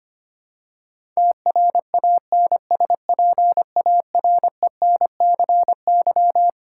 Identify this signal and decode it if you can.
Morse code, heard as a keyed tone, about 25 wpm, 700 Hz. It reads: TRANSPARENCY
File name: Twelve Letter Words - 25wpm